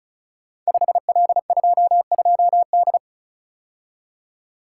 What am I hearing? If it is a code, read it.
Morse code, heard as a keyed tone, about 36 wpm, 700 Hz. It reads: 5L22D
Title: Code Groups - 5 Letter:Numbers - 35wpm